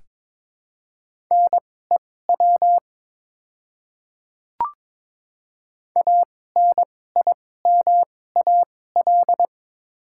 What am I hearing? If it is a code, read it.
Morse code, heard as a keyed tone, about 22 wpm, spaced as for 16 wpm, 700 Hz. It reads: NEW ANIMAL
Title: 2x Character Spacing - Top 200 Words - 22wpm